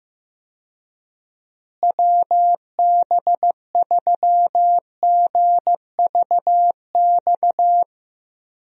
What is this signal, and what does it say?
Morse code, heard as a keyed tone, about 15 wpm, 700 Hz. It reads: WB3GVX
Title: USA Call Signs - 6 Characters - Repeated 2x - 15wpm